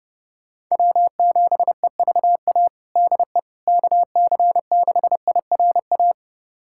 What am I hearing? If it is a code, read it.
Morse code, heard as a keyed tone, about 30 wpm, 700 Hz. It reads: W7EVA DE KC6IRA
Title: USA Callsign DE Callsign 30wpm